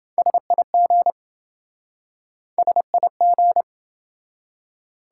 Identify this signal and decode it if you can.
Morse code, heard as a keyed tone, about 27 wpm, 700 Hz. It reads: SIG SIG